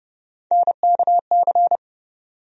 Morse code, heard as a keyed tone, about 30 wpm, 700 Hz. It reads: NKC
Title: USA Call Sign Postfixes - 3 Letters 30wpm